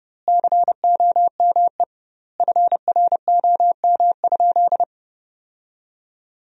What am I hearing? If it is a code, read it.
Morse code, heard as a keyed tone, about 30 wpm, 700 Hz. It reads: COME FROM?
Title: 2x Word-Spacing - Sentences from Top 200 Words - 30wpm